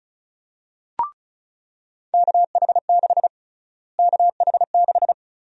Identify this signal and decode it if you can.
Morse code, heard as a keyed tone, about 34 wpm, 700 Hz. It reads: KH6 KH6
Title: USA Call Sign Prefixes - Repeated 2x - 35wpm